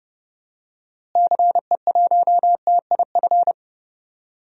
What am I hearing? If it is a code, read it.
Morse code, heard as a keyed tone, about 30 wpm, 700 Hz. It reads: CE1TIF
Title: DXCC-100 Call Signs - 6 Characters - Repeated 2x - 30wpm